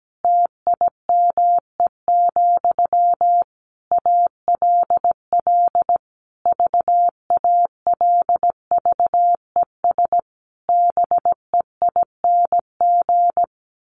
Morse code, heard as a keyed tone, about 17 wpm, 700 Hz. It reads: TIME, ALL VALVES BEING
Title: A Sound Of Thunder 17wpm